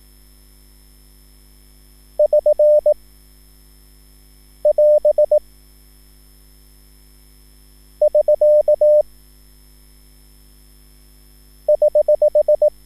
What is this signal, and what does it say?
Morse code, heard as a keyed tone, about 18 wpm, 600 Hz. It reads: <SN> <AS> <SK> <HH>
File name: QSL / CW lesson